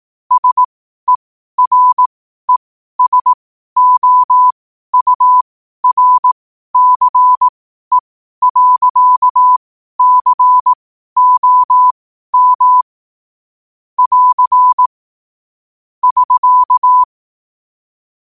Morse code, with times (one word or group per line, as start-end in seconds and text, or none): 0.3-12.8 SERESOURCE.COM
14.0-14.9 <AR>
16.0-17.0 <SK>